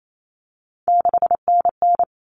Morse code, heard as a keyed tone, about 28 wpm, 700 Hz. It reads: 6NN
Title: Common Cut Numbers - 3-Digit 28wpm